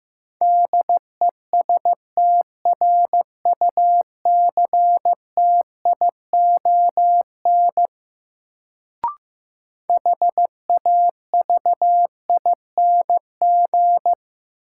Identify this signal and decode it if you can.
Morse code, heard as a keyed tone, about 15 wpm, 700 Hz. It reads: DESTRUCTION HAVING
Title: Sets of 5 Words Encore - New Word Drill - 15wpm